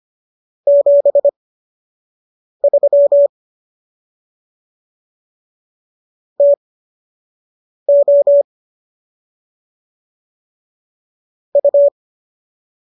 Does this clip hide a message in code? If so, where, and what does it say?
Yes: Morse code from 0.7 to 11.9 s, heard as a keyed tone, about 25 wpm, spaced as for 6 wpm, 570 Hz. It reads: 73 TO U